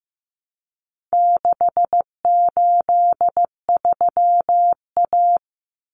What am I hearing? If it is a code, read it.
Morse code, heard as a keyed tone, about 15 wpm, 700 Hz. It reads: 683A
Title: Common Cut Numbers - 4-Digit 15wpm